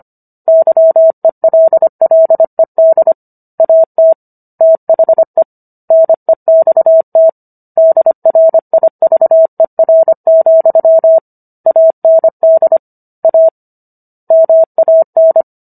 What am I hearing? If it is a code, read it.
Morse code, heard as a keyed tone, about 25 wpm, 650 Hz. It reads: YELLED AT THE NEXT DRIVER, AND A MAN